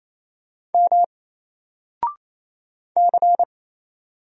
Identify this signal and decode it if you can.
Morse code, heard as a keyed tone, about 28 wpm, 700 Hz. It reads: M C
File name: Single-Letter-Number-28wpm